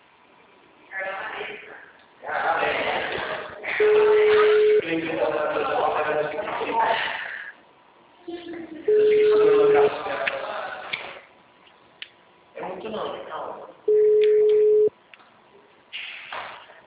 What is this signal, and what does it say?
Ringback tone (Europe, 425 Hz), with speech.